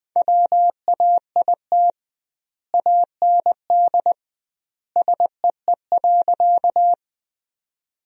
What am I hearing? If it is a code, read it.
Morse code, heard as a keyed tone, about 21 wpm, 700 Hz. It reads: WAIT AND SEE.
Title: 2x Word-Spacing - Sentences from Top 500 Words - 20wpm